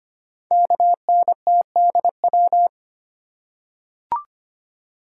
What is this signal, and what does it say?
Morse code, heard as a keyed tone, about 25 wpm, 700 Hz. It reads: KNTDW
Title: Code Groups - 5 Letters - 25wpm